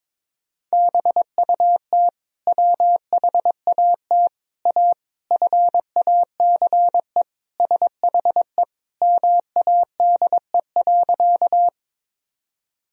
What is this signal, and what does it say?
Morse code, heard as a keyed tone, about 22 wpm, 700 Hz. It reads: BUT WHAT A FACE SHE MADE.
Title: Sentences from Top 1000 Words - Review Part 2 - 22wpm